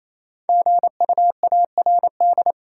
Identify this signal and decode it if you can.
Morse code, heard as a keyed tone, about 28 wpm, 700 Hz. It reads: GUARD
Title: Five Letter Words - 28wpm